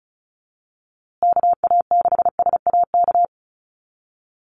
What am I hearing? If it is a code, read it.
Morse code, heard as a keyed tone, about 34 wpm, 700 Hz. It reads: KA6SAK
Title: USA Call Signs Repeated 2x 35wpm